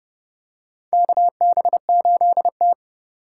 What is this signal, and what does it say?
Morse code, heard as a keyed tone, about 32 wpm, 700 Hz. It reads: KB8T
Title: USA Call Signs - 4 Characters - 3x Character Spacing - 30wpm